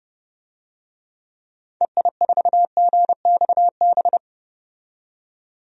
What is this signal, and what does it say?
Morse code, heard as a keyed tone, about 30 wpm, 700 Hz. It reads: EI4GXB
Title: DXCC-100 Call Signs - 6 Characters - 30wpm